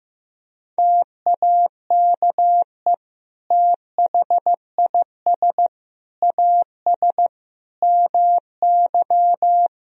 Morse code, heard as a keyed tone, about 15 wpm, 700 Hz. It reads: TAKE THIS AS MY
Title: Sentences from Top 1000 Words - Review Part 2 - 15wpm